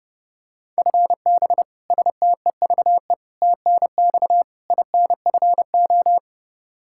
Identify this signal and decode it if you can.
Morse code, heard as a keyed tone, about 28 wpm, 700 Hz. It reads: FB STEVE TNX INFO
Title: Practice QSO - Exchange 3 - More Info - 30wpm